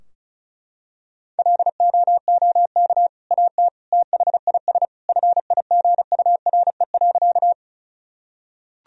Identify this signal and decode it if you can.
Morse code, heard as a keyed tone, about 36 wpm, 700 Hz. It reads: LOOK AT THIS FIGURE.
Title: Sentences from Top 1000 Words - Review Part 2 - 35wpm